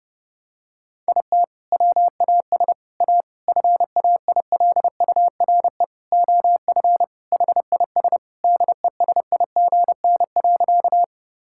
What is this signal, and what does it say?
Morse code, heard as a keyed tone, about 32 wpm, 700 Hz. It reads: IT WAS A FAILURE OF HIS DESIGN.
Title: Sentences from Top 1000 Words - Review Part 6 - 30wpm